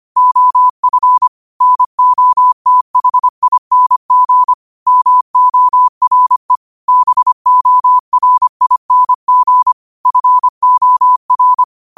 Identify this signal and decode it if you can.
Morse code, heard as a keyed tone, about 25 wpm, 1000 Hz. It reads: OF NOTHING MORE BORING FOR